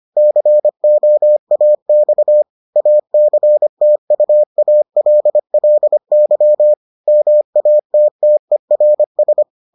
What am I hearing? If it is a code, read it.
Morse code, heard as a keyed tone, about 25 wpm, 600 Hz. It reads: COAX ACTUALLY MATTERS